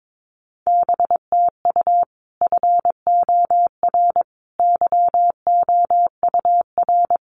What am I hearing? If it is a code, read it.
Morse code, heard as a keyed tone, about 22 wpm, 700 Hz. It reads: BTU FOR YOUR
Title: Common QSO Phrases 22wpm